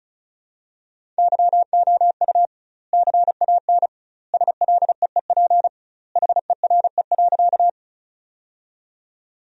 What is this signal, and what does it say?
Morse code, heard as a keyed tone, about 36 wpm, 700 Hz. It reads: YOU CAN SLEEP HERE.
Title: 2x Word-Spacing - Sentences from Top 500 Words - 35wpm